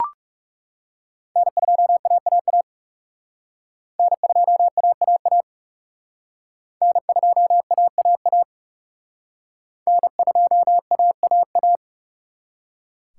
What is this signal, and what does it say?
Morse code, heard as a keyed tone, about 37 wpm, 700 Hz. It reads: N2AAA N2AAA N2AAA N2AAA
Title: Speed Racing - USA 1x3 Call Signs 30wpm